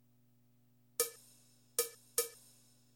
There is no code.